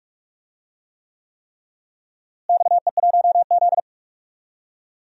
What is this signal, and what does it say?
Morse code, heard as a keyed tone, about 46 wpm, 700 Hz. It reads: XE1Z
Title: DXCC-100 Call Signs - 4 Characters - Repeated 3x - 45wpm